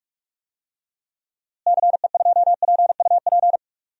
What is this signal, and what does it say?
Morse code, heard as a keyed tone, about 42 wpm, 700 Hz. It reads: CE2PUP
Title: DXCC-100 Call Signs - 6 Characters - Repeated 3x - 45wpm